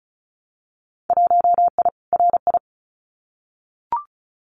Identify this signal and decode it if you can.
Morse code, heard as a keyed tone, about 34 wpm, 700 Hz. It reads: 1I RI
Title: Winter Field Day Exchanges Encore 35wpm